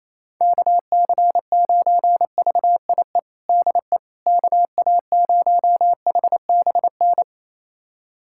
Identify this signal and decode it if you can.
Morse code, heard as a keyed tone, about 27 wpm, 700 Hz. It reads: KC9VIE DE KA0HBN
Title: USA Callsign DE Callsign - 6 Characters - 28wpm